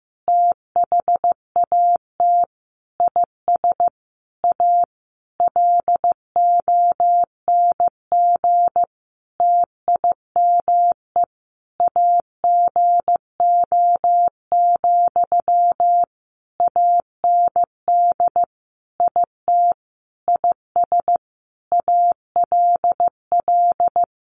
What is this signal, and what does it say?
Morse code, heard as a keyed tone, about 14 wpm, 700 Hz. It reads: THAT IS A LONG TIME AGO, AND IT IS ALL
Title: Sentences from Top 600 Words 15wpm